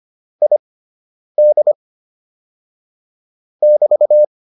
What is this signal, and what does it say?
Morse code, heard as a keyed tone, about 25 wpm, spaced as for 9 wpm, 600 Hz. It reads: ID <BT>
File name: QSO 105 09